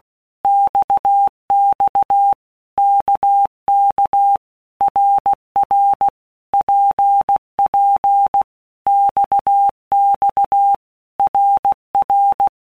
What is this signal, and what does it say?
Morse code, heard as a keyed tone, about 16 wpm, 800 Hz. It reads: XX KK RR PP XX RR
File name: Basic Letter Groups By Blocks / Groups@15 / RKPX@15